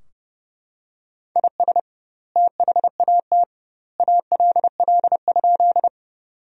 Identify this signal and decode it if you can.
Morse code, heard as a keyed tone, about 32 wpm, 700 Hz. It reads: IS THAT ALL?
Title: 2x Word-Spacing - Sentences from Top 100 Words - 30wpm